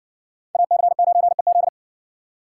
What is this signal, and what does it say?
Morse code, heard as a keyed tone, about 58 wpm, 700 Hz. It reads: AC9EZ